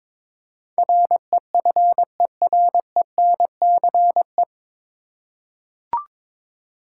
Morse code, heard as a keyed tone, about 22 wpm, 700 Hz. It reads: REFERENCE
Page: Morse Code Ninja practice 22wpm